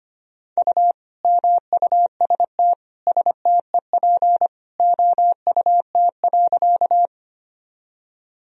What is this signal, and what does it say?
Morse code, heard as a keyed tone, about 25 wpm, 700 Hz. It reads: U MUST STEP OUT.
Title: Sentences from Top 400 Words and Abbreviations 25wpm